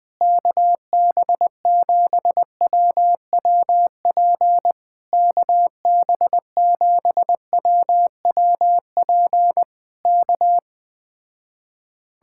Morse code, heard as a keyed tone, about 20 wpm, 700 Hz. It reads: KB7WWP KB7WWP K
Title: Speed-Racing Common QSO Phrases - 17wpm